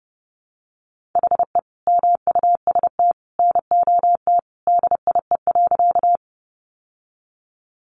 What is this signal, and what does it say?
Morse code, heard as a keyed tone, about 30 wpm, 700 Hz. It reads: HE MUST NOT DIE.
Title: Sentences from Top 1300 Words 30wpm